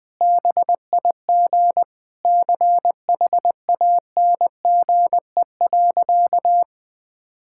Morse code, heard as a keyed tone, about 21 wpm, 700 Hz. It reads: BIG CHANGE.
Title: Sentences with 7-Word Phrases - Spoken Beginning - 20wpm